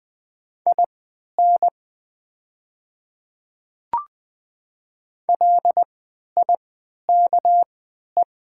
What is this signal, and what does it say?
Morse code, heard as a keyed tone, about 20 wpm, spaced as for 11 wpm, 700 Hz. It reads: IN LIKE
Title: 3x Character Spacing - Top 100 Words - 20wpm